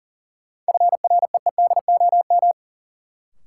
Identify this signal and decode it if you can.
Morse code, heard as a keyed tone, about 40 wpm, 700 Hz. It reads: FREEDOM
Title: Sentences from Top 800 Words - New Word Drill - 40wpm